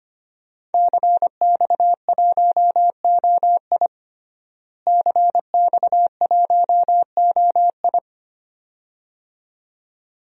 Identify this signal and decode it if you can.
Morse code, heard as a keyed tone, about 25 wpm, 700 Hz. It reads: CX1OI CX1OI
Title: DXCC-100 Call Signs - 5 Characters - Repeated 2x - 25wpm